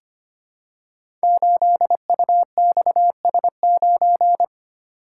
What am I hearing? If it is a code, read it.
Morse code, heard as a keyed tone, about 25 wpm, 700 Hz. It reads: 8UXS9
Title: Code Groups - 5 Letter:Numbers - 25wpm